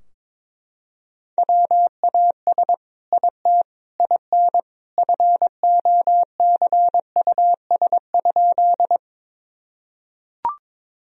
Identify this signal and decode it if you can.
Morse code, heard as a keyed tone, about 22 wpm, 700 Hz. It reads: WAS IT IN FOCUS?
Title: Sentences from Top 1000 Words - Review Part 4 - 22wpm